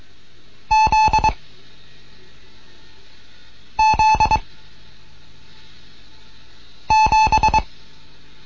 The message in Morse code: Z Z 7